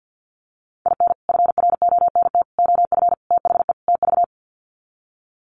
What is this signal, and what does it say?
Morse code, heard as a keyed tone, about 50 wpm, 700 Hz. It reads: IN FRONT OF THE TV
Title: Sets of 5 Words Encore2 50wpm